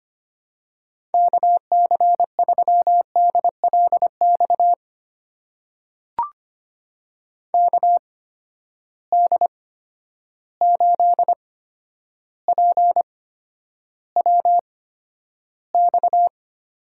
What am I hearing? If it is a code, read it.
Morse code, heard as a keyed tone, about 25 wpm, spaced as for 11 wpm, 700 Hz. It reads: KC3DLX KD8PWX
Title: USA Call Signs - 6 Characters - 8x Character Spacing - 25wpm